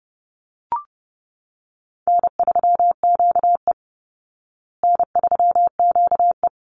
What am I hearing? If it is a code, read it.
Morse code, heard as a keyed tone, about 30 wpm, 700 Hz. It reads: N3QE N3QE
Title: Easy Call Signs Repeated 2x - 30wpm